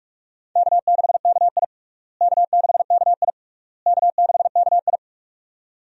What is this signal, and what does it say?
Morse code, heard as a keyed tone, about 46 wpm, 700 Hz. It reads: K6KI K6KI K6KI
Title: USA Call Signs - 4 Characters - Repeated 3x - 45wpm